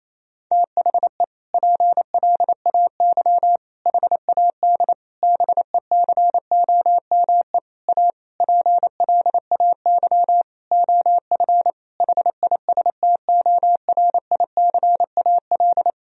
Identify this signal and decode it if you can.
Morse code, heard as a keyed tone, about 27 wpm, 700 Hz. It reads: THE PLAY HAD BECOME A PLAY OF HISTORICAL
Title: Sentences from Top 1000 Words - Review Part 4 - 28wpm